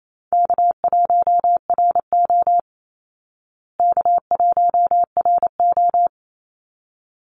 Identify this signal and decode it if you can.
Morse code, heard as a keyed tone, about 28 wpm, 700 Hz. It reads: K1RO K1RO
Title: Call Signs Repeated 3x - 28wpm